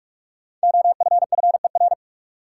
Morse code, heard as a keyed tone, about 44 wpm, 700 Hz. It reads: OFFER
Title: Five Letter Words - 45wpm